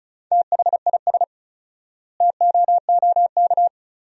Morse code, heard as a keyed tone, about 36 wpm, 700 Hz. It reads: THIS TOOK
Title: 4x Word-Spacing - Sentences from Top 300 Words - 35wpm